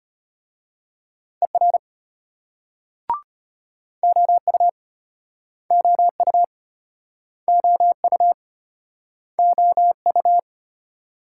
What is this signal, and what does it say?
Morse code, heard as a keyed tone, about 35 wpm, 700 Hz. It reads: ER OU OU OU OU